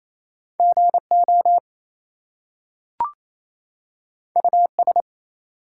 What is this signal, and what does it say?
Morse code, heard as a keyed tone, about 28 wpm, 700 Hz. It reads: GO US
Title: Two Letter Words - 28wpm